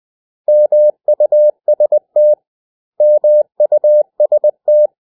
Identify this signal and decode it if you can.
Morse code, heard as a keyed tone, about 20 wpm, 600 Hz. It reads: MUST MUST
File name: cw / 100 Common English Words